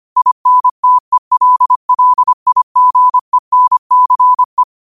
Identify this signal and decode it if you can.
Morse code, heard as a keyed tone, about 25 wpm, 1000 Hz. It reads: INTELLIGENCE
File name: qotdfiles / 25wpm